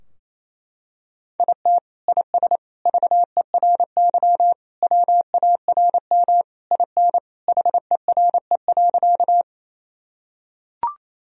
Morse code, heard as a keyed tone, about 27 wpm, 700 Hz. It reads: IT IS VERY WARM IN HERE.
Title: Sentences from Top 1000 Words - Review Part 2 - 28wpm